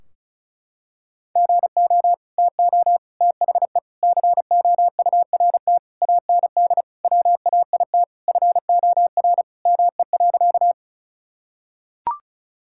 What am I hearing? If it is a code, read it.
Morse code, heard as a keyed tone, about 36 wpm, 700 Hz. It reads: GO TO THE COURT AND WAIT FOR ME.
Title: Sentences from Top 1600 Words 35wpm